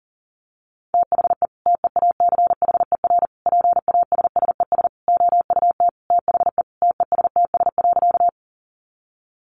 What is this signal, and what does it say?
Morse code, heard as a keyed tone, about 40 wpm, 700 Hz. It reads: THE TEACHER PASSES OUT THE TESTS.
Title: Additional Easy Phrases 40wpm